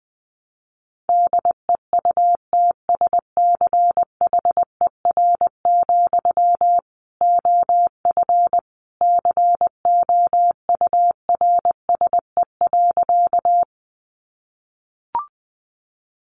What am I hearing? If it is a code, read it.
Morse code, heard as a keyed tone, about 20 wpm, 700 Hz. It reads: DEUTSCHER, OF COURSE.
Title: A Sound Of Thunder 20wpm